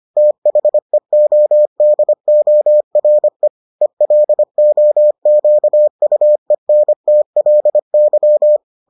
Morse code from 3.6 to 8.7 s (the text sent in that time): ELOQUENTLY